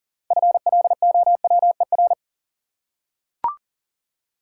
Morse code, heard as a keyed tone, about 38 wpm, 700 Hz. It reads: FLOWER
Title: Six Letter Words - 40wpm